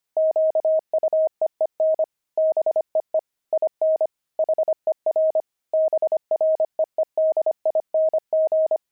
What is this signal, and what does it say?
Morse code, heard as a keyed tone, about 25 wpm, 630 Hz. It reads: QUEEN BEE IN HER BREEDING
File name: pangram-fs-25-cs-25-beehive